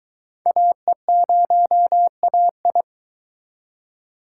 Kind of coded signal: Morse code

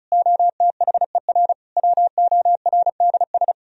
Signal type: Morse code

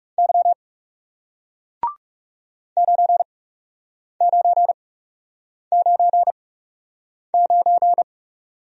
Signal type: Morse code

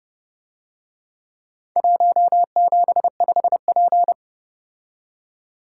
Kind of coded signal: Morse code

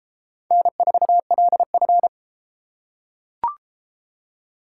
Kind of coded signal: Morse code

